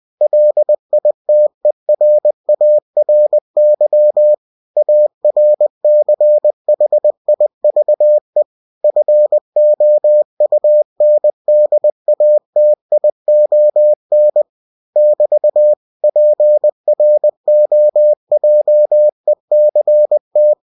Morse code, heard as a keyed tone, about 20 wpm, 600 Hz. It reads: LITERARY ARCHIVE FOUNDATION <BT> PROJECT
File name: cw-books / winnie-the-pooh / winnie-20wpm-page-0121